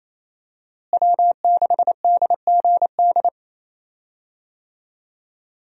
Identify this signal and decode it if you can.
Morse code, heard as a keyed tone, about 28 wpm, 700 Hz. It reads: W6DGD
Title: USA Call Signs - 5 Characters - Repeated 3x - 28wpm